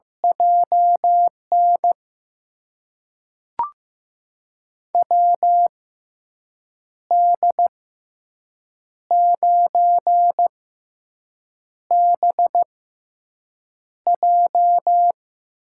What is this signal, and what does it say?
Morse code, heard as a keyed tone, about 15 wpm, spaced as for 6 wpm, 700 Hz. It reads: JN WD9BJ
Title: USA Call Signs - 6 Characters - 6x Character Spacing - 15wpm